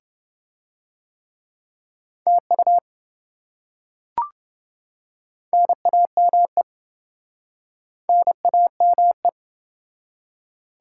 Morse code, heard as a keyed tone, about 26 wpm, 700 Hz. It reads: TU NAME NAME